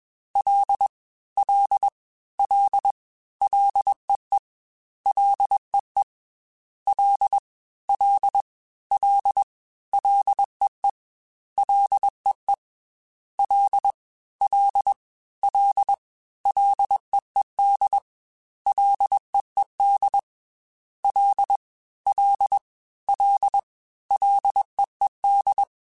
Morse code, heard as a keyed tone, about 21 wpm, 800 Hz. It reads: L L L LEE LEE L L L LEE LEE L L L LEED LEED L L L LEED